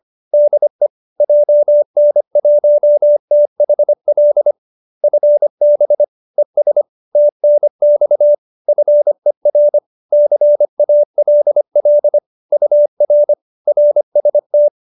Morse code, heard as a keyed tone, about 25 wpm, 590 Hz. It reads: DE JN1THL FB ES TNX FER CALL UR RST